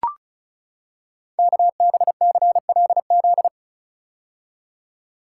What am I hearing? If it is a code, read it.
Morse code, heard as a keyed tone, about 36 wpm, 700 Hz. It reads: KBCLZ
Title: Code Groups - 5 Letters - 35wpm